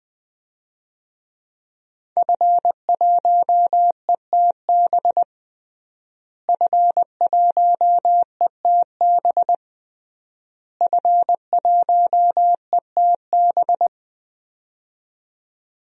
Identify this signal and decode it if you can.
Morse code, heard as a keyed tone, about 20 wpm, 700 Hz. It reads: F1ETB F1ETB F1ETB
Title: DXCC-100 Call Signs - 5 Characters - Repeated 3x - 20wpm